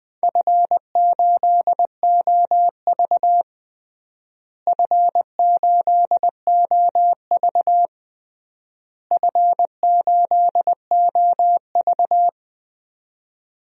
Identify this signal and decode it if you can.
Morse code, heard as a keyed tone, about 20 wpm, 700 Hz. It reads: F8OV F8OV F8OV
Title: DXCC-100 Call Signs - 4 Characters - Repeated 3x - 20wpm